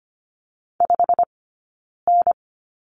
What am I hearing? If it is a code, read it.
Morse code, heard as a keyed tone, about 25 wpm, 700 Hz. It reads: 5 N
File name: course / 25:10wpm / 092 Lesson 16d - Letter c - Two Characters 25wpm